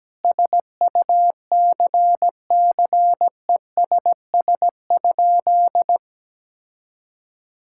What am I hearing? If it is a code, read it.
Morse code, heard as a keyed tone, about 17 wpm, 700 Hz. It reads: SUCCESS?
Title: Sentences from Top 1000 Words - Review Part 6 - 17wpm